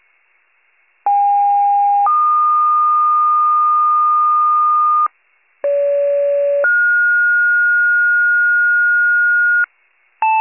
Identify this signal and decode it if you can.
Two-tone page, heard as a one-second tone, then a three-second tone, about 810 then 1220 Hz, about 570 then 1430 Hz, with noise. It is heard from 1.1 to 9.6 s.